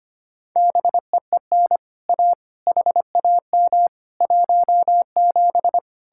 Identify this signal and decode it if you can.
Morse code, heard as a keyed tone, about 31 wpm, 700 Hz. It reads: BEEN A HAM 17
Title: Speed-Racing Common QSO Phrases - 25wpm